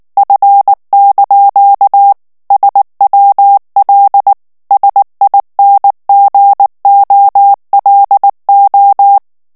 Morse code, heard as a keyed tone, about 19 wpm, 800 Hz. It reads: F) SWL SINGOLO
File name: morse mp3 / ITALIEN texte clair 2 vitesse 18 mots minute